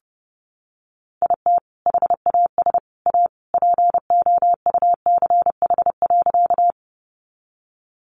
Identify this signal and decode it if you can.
Morse code, heard as a keyed tone, about 28 wpm, 700 Hz. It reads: IT HAS A POUCH.
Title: Easy Phrases 30wpm